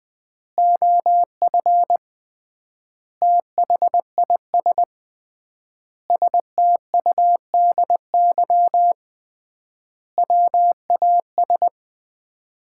Morse code, heard as a keyed tone, about 20 wpm, 700 Hz. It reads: OF THIS STUDY WAS
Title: 3x Word-Spacing - Sets of 5 Words - 20wpm